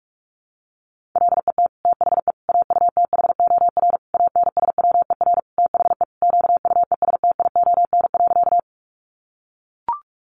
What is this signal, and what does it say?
Morse code, heard as a keyed tone, about 44 wpm, 700 Hz. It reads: LET THE AUTHOR ANSWER THE QUESTION.
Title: Sentences from Top 1200 Words 45wpm